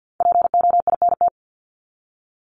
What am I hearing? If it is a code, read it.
Morse code, heard as a keyed tone, about 50 wpm, 700 Hz. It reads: POINT
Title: ES:S Suffix - Top 100 Words with Base Words 50wpm